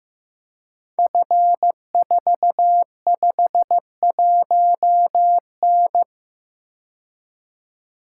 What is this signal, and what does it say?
Morse code, heard as a keyed tone, about 15 wpm, 700 Hz. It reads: F451N
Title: Code Groups - 5 Letter:Numbers - 15wpm